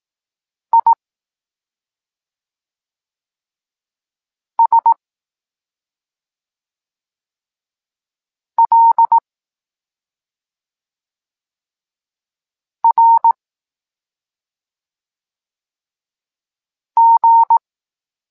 Morse code, heard as a keyed tone, about 18 wpm, 920 Hz. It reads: I S L R G
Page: Tekst 23.3 – Ciąg dalszy doskonalenia odbioru poznanych liter